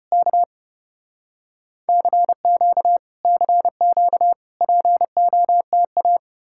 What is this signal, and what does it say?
Morse code, heard as a keyed tone, about 33 wpm, 700 Hz. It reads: K CQ CQ POTA